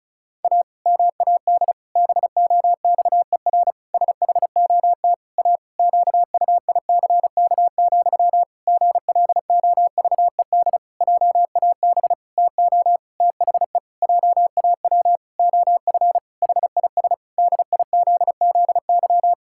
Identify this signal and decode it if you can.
Morse code, heard as a keyed tone, about 34 wpm, 700 Hz. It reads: A MAD BOXER SHOT A QUICK, GLOVED JAB TO THE JAW OF HIS DIZZY
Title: Pangram Challenge v2 - 35wpm